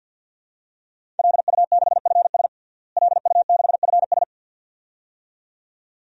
Morse code, heard as a keyed tone, about 53 wpm, 700 Hz. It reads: LU6FS LU6FS
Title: DXCC-100 Call Signs - 5 Characters - Repeated 2x - 50wpm